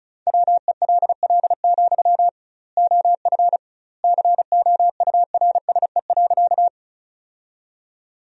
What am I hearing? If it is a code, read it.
Morse code, heard as a keyed tone, about 34 wpm, 700 Hz. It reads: WELL, OF COURSE.
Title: 2x Word-Spacing - Sentences from Top 500 Words - 35wpm